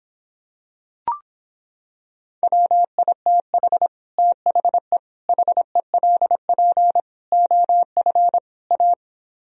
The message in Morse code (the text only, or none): WITH THE HELP OF A